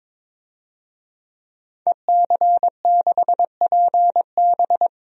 Morse code, heard as a keyed tone, about 22 wpm, 700 Hz. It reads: EC6PB
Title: DXCC-100 Call Signs - 5 Characters - 22wpm